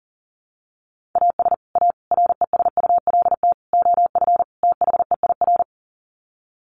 Morse code, heard as a keyed tone, about 40 wpm, 700 Hz. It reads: AS A RESULT OF THEIR
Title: Sets of 5 Words Encore2 40wpm